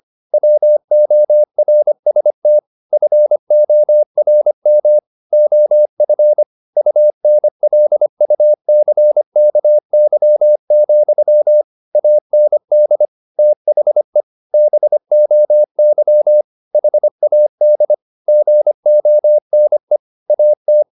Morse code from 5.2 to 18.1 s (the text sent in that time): OF UNLUCKY, AND THE BOY HAD